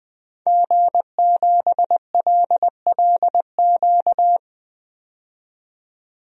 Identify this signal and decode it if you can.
Morse code, heard as a keyed tone, about 20 wpm, 700 Hz. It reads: G7LLQ
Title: DXCC-100 Call Signs - 5 Characters - 20wpm